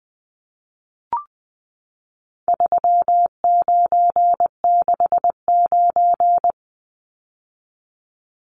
Morse code, heard as a keyed tone, about 19 wpm, 700 Hz. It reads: 3969
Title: Numbers - 4-Digit 20wpm